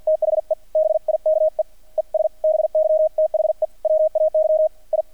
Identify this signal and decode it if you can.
Morse code, heard as a keyed tone, about 48 wpm, 630 Hz, with radio noise. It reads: THE BIKE ES BY THE WAY I
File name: archived / CRSnet